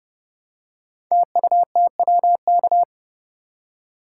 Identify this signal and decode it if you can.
Morse code, heard as a keyed tone, about 30 wpm, 700 Hz. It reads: TUTWK
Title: Code Groups - 5 Letters - 30wpm